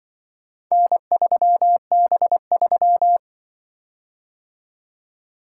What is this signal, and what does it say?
Morse code, heard as a keyed tone, about 24 wpm, 700 Hz. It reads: N3B3
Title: Speed-Racing - Four Letter-Number Combinations - 20wpm